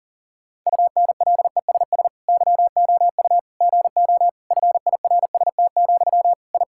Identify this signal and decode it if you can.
Morse code, heard as a keyed tone, about 40 wpm, 700 Hz. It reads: UNLESS YOU GO FIRST, I
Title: Sentences from Top 1000 Words - Review Part 6 - 40wpm